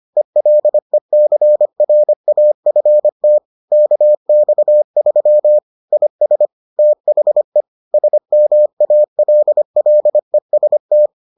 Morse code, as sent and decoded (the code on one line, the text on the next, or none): . .-.. -.-. .-. .- ..-. - / -.- -..- ...-- / .. ... / - .... / ... -- .- .-.. .-.. ... -
ELECRAFT KX3 IS THE SMALLEST